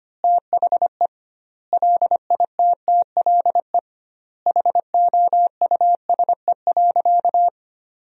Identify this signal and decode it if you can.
Morse code, heard as a keyed tone, about 25 wpm, 700 Hz. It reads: THE LITTLE HOUSE.
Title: 2x Word-Spacing - Sentences from Top 200 Words - 25wpm